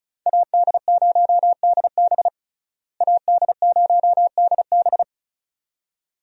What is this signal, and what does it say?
Morse code, heard as a keyed tone, about 34 wpm, 700 Hz. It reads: AD0DB AD0DB
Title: USA Call Signs - 5 Characters - Repeated 2x - 35wpm